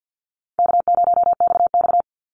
Morse code, heard as a keyed tone, about 50 wpm, 700 Hz. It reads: K0XX
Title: Easy Call Signs - 50wpm